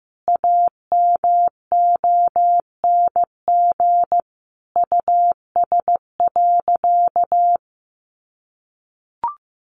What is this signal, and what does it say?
Morse code, heard as a keyed tone, about 14 wpm, 700 Hz. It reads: AMONG US.
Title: Sentences from Top 500 Words 15wpm